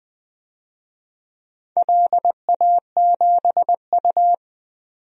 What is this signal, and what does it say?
Morse code, heard as a keyed tone, about 20 wpm, 700 Hz. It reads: LA7U
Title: DXCC-100 Call Signs - 4 Characters - 20wpm